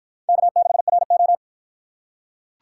Morse code, heard as a keyed tone, about 46 wpm, 700 Hz. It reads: K6RO